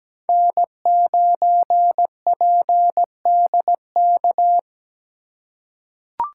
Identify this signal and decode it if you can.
Morse code, heard as a keyed tone, about 17 wpm, 700 Hz. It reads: N9PDK
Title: USA Call Signs - 5 Characters - 2x Character Spacing - 17wpm